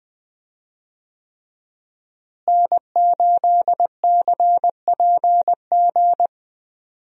Morse code, heard as a keyed tone, about 20 wpm, 700 Hz. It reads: N8CPG
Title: USA Call Signs - 5 Characters - Repeated 3x - 20wpm